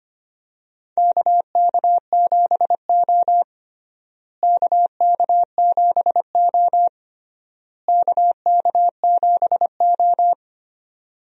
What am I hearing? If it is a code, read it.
Morse code, heard as a keyed tone, about 25 wpm, 700 Hz. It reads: KK7O KK7O KK7O
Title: USA Call Signs - 4 Characters - Repeated 3x - 25wpm